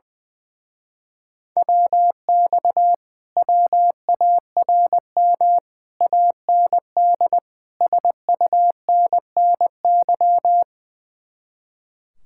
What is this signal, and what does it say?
Morse code, heard as a keyed tone, about 20 wpm, 700 Hz. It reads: WX WARM AND SUNNY
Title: Speed-Racing Common QSO Phrases Encore - 20wpm